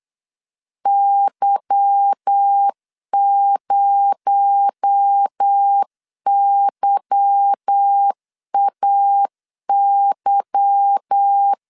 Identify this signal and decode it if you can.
Morse code, heard as a keyed tone, about 8 wpm, 800 Hz. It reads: Y0YAY
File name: LEZIONI / 8 YJ45 / 8 Y 8wpm 3245